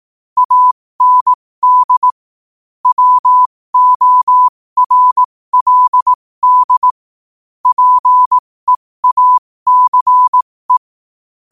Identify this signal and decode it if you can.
Morse code, heard as a keyed tone, about 18 wpm, spaced as for 15 wpm, 1000 Hz. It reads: AND WORLD PEACE
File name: qotdfiles / 15wpm